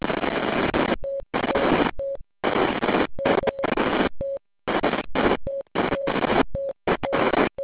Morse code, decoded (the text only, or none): none